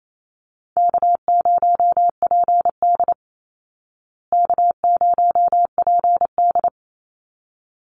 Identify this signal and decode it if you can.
Morse code, heard as a keyed tone, about 28 wpm, 700 Hz. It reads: K0PD K0PD
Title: Easy Call Signs Repeated 2x - 28wpm